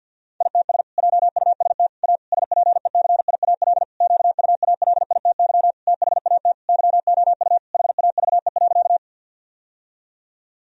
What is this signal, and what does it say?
Morse code, heard as a keyed tone, about 50 wpm, 700 Hz. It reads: ITS JUST A SPECIAL QUALITY THAT YOU HAVE.
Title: Sentences from Top 1000 Words - Review Part 4 - 50wpm